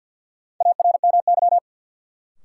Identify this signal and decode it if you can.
Morse code, heard as a keyed tone, about 46 wpm, 700 Hz. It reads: ARMY
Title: Sentences from Top 800 Words - New Word Drill - 50wpm